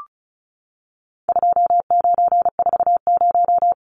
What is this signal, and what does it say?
Morse code, heard as a keyed tone, about 34 wpm, 700 Hz. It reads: 2940
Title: Numbers - 4-Digit 35wpm